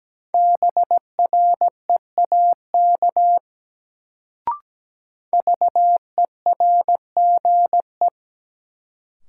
Morse code, heard as a keyed tone, about 17 wpm, 700 Hz. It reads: BREAK VERGE
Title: Sets of 4 Words - New Word Drill - 17wpm